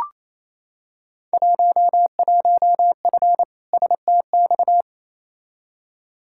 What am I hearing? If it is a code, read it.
Morse code, heard as a keyed tone, about 28 wpm, 700 Hz. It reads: 11F STX
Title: Field Day Exchanges Encore - 28wpm